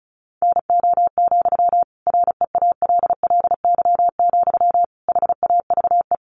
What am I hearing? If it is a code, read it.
Morse code, heard as a keyed tone, about 34 wpm, 700 Hz. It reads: NO, REALLY, HAVE
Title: Sentences from Top 1000 Words 35wpm